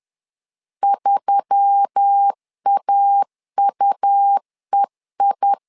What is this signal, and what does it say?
Morse code, heard as a keyed tone, about 10 wpm, 800 Hz. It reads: 3AUEI
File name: LEZIONI / 5 LBVC / 5 L 10wpm 2259